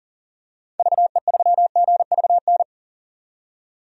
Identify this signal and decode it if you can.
Morse code, heard as a keyed tone, about 40 wpm, 700 Hz. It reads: VE3GVN
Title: DXCC-100 Call Signs - 6 Characters - 40wpm